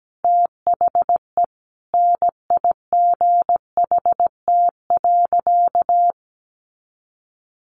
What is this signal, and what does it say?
Morse code, heard as a keyed tone, about 17 wpm, 700 Hz. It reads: THE NIGHT.
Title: Sentences from Top 1400 Words 17wpm